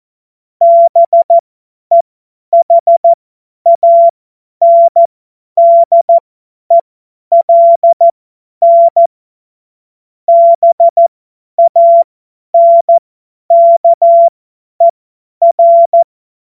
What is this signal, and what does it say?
Morse code, heard as a keyed tone, about 14 wpm, spaced as for 10 wpm, 680 Hz. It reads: BEHANDELN BANKER
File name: Witz-15-Juli-2025-50-F70